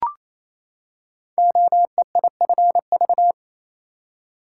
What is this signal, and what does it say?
Morse code, heard as a keyed tone, about 28 wpm, 700 Hz. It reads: OEIFV
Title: Code Groups - 5 Letters - 28wpm